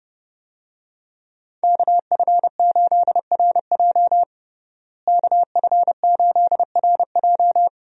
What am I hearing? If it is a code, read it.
Morse code, heard as a keyed tone, about 30 wpm, 700 Hz. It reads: KF8RJ KF8RJ
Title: USA Call Signs - 5 Characters - Repeated 3x - 30wpm